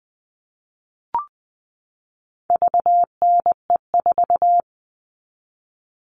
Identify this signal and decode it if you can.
Morse code, heard as a keyed tone, about 20 wpm, 700 Hz. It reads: VNE4
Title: All Cut Numbers - 4-Digit 20wpm